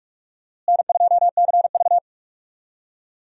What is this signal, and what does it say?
Morse code, heard as a keyed tone, about 46 wpm, 700 Hz. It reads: N2CV
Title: USA Call Signs - 4 Characters - Repeated 2x - 45wpm